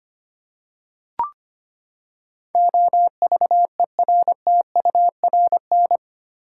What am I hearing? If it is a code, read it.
Morse code, heard as a keyed tone, about 25 wpm, 700 Hz. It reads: OVERTURN
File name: Eight Letter Words - 25wpm